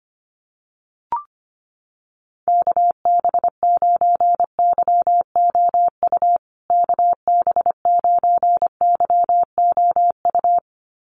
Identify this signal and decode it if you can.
Morse code, heard as a keyed tone, about 25 wpm, 700 Hz. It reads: KB9YOU KB9YOU
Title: USA Call Signs Repeated 2x 25wpm